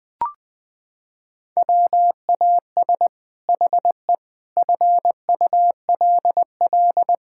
Morse code, heard as a keyed tone, about 20 wpm, 700 Hz. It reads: WAS HE FULL
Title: Sentences from Top 1000 Words - Review Part 6 - 20wpm